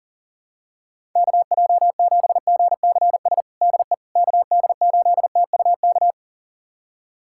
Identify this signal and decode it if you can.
Morse code, heard as a keyed tone, about 40 wpm, 700 Hz. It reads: KJ7GCS DE KD8TUK
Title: USA Callsign DE Callsign 40wpm